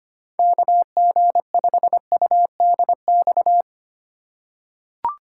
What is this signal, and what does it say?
Morse code, heard as a keyed tone, about 25 wpm, 700 Hz. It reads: KG5UDX
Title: USA Call Signs - 6 Characters - 2x Character Spacing - 25wpm